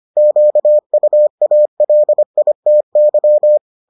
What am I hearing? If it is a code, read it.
Morse code, heard as a keyed tone, about 25 wpm, 600 Hz. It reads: QUALITY